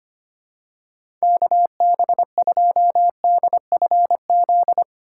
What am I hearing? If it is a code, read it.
Morse code, heard as a keyed tone, about 25 wpm, 700 Hz. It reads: KB2DFZ
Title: USA Call Signs - 6 Characters - 3x Character Spacing - 25wpm